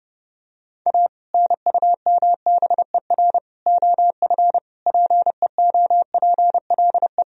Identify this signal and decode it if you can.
Morse code, heard as a keyed tone, about 30 wpm, 700 Hz. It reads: A NUMBER OF PEOPLE
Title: Sentences from Top 1000 Words - Review Part 1 - 30wpm